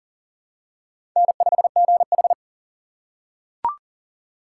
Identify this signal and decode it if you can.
Morse code, heard as a keyed tone, about 38 wpm, 700 Hz. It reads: N5GH
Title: USA Call Signs - 4 Characters - 8x Character Spacing - 40wpm